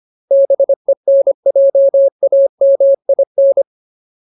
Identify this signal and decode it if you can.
Morse code, heard as a keyed tone, about 25 wpm, 550 Hz. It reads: BENJAMIN
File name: Ultimate Word List / 25WPM Ultimate